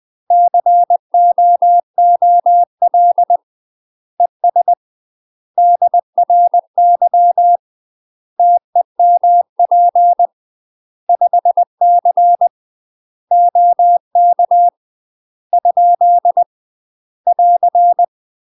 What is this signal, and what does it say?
Morse code, heard as a keyed tone, about 20 wpm, 700 Hz. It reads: COOL ES DRY TEMP 5C OK ? <AR>
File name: UK MASTER QSOs TRAINING / 20WPM / UK MASTER QSOs 0050